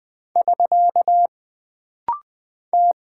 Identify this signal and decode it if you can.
Morse code, heard as a keyed tone, about 20 wpm, 700 Hz. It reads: <SK> T
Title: Single Character (Letter:Number:Punctuation:Prosign) - Rapid-Fire - 20wpm